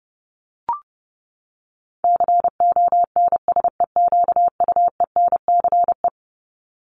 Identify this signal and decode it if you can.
Morse code, heard as a keyed tone, about 30 wpm, 700 Hz. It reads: CONSEQUENCE
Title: LY Suffix - Top 100 Words with Base Words 30wpm